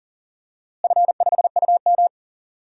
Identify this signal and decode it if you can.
Morse code, heard as a keyed tone, about 40 wpm, 700 Hz. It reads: F5UM
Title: DXCC-100 Call Signs - 4 Characters - Repeated 3x - 40wpm